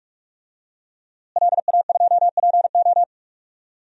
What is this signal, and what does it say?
Morse code, heard as a keyed tone, about 46 wpm, 700 Hz. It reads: LA2PO
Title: DXCC-100 Call Signs - 5 Characters - Repeated 3x - 45wpm